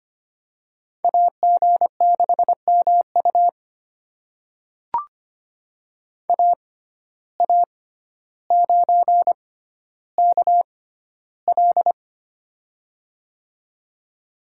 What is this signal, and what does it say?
Morse code, heard as a keyed tone, about 25 wpm, spaced as for 13 wpm, 700 Hz. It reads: AG6MU AA9KL
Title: USA Call Signs - 5 Characters - 6x Character Spacing - 25wpm